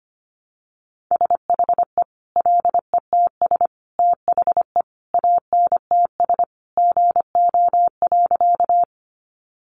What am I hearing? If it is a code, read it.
Morse code, heard as a keyed tone, about 25 wpm, 700 Hz. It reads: SHE LETS THE ANTS GO.
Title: Additional Easy Phrases 25wpm